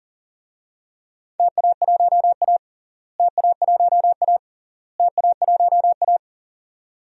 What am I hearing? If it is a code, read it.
Morse code, heard as a keyed tone, about 40 wpm, 700 Hz. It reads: TA1A TA1A TA1A
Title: DXCC-100 Call Signs - 4 Characters - Repeated 3x - 40wpm